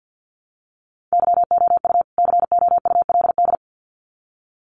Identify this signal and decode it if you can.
Morse code, heard as a keyed tone, about 48 wpm, 700 Hz. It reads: YOU COULD
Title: Sets of 2 Words 50wpm